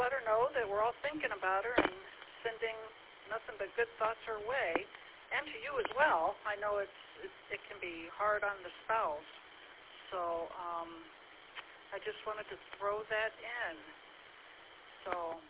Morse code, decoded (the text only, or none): none